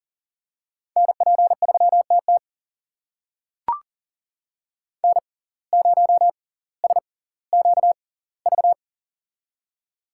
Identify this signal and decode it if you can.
Morse code, heard as a keyed tone, about 40 wpm, spaced as for 20 wpm, 700 Hz. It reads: NP3TT N0SQV
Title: USA Call Signs - 5 Characters - 6x Character Spacing - 40wpm